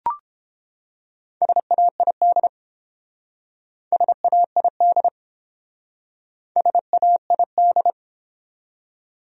Morse code, heard as a keyed tone, about 30 wpm, 700 Hz. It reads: SAID SAID SAID